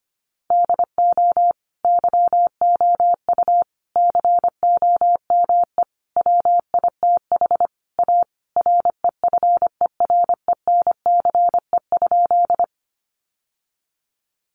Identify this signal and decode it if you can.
Morse code, heard as a keyed tone, about 25 wpm, 700 Hz. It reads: DO YOU COME WITH A REFERENCE?
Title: Sentences from Top 1200 Words 25wpm